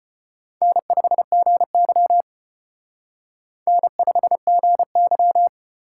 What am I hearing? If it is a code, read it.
Morse code, heard as a keyed tone, about 31 wpm, 700 Hz. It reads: N5GY N5GY
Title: Speed Racing - USA 1x2 Call Signs 25wpm